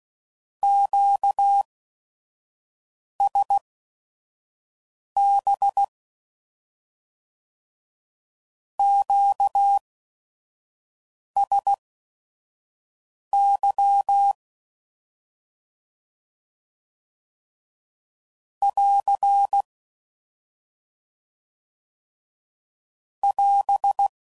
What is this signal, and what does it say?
Morse code, heard as a keyed tone, about 16 wpm, spaced as for 5 wpm, 800 Hz. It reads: QSB QSY <AR> <AS>
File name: CW Code Groups @5WPM / Q Calls / Q Calls Once